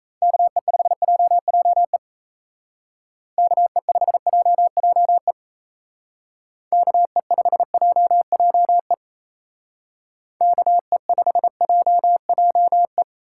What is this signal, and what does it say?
Morse code, heard as a keyed tone, about 34 wpm, 700 Hz. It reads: KE5JJE KE5JJE KE5JJE KE5JJE
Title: Speed Racing - USA 2x3 Call Signs 28wpm